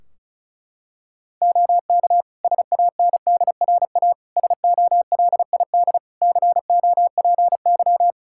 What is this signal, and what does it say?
Morse code, heard as a keyed tone, about 36 wpm, 700 Hz. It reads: OK SANDRA SOLID COPY
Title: Practice QSO - Exchange 3 - More Info - 35wpm